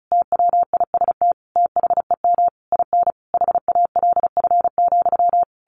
Morse code, heard as a keyed tone, about 34 wpm, 700 Hz. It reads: TWIST THEM IN HALF,
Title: A Sound Of Thunder 35wpm